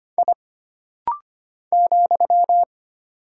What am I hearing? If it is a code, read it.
Morse code, heard as a keyed tone, about 25 wpm, 700 Hz. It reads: I ,
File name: Single Character (Letter:Number:Punctuation:Prosign) - Rapid-Fire - 25wpm